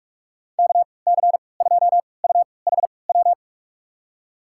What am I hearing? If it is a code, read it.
Morse code, heard as a keyed tone, about 41 wpm, spaced as for 25 wpm, 700 Hz. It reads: KC2UHW
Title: USA Call Signs - 6 Characters - 3x Character Spacing - 45wpm